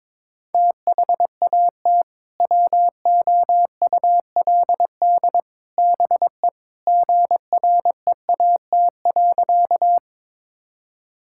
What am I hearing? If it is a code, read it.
Morse code, heard as a keyed tone, about 22 wpm, 700 Hz. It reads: THAT WOULD BE GREAT.
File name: practice / 22wpm / Sentences with 3-Word Phrases - 22wpm